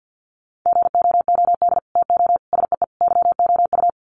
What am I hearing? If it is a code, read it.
Morse code, heard as a keyed tone, about 50 wpm, 700 Hz. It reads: GOOD TO SEE YOU
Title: Sets of 4 Words 50wpm